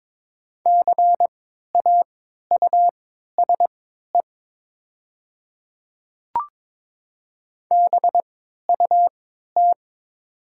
Morse code, heard as a keyed tone, about 22 wpm, spaced as for 12 wpm, 700 Hz. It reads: CAUSE BUT
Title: 3x Character Spacing - Top 200 Words - 22wpm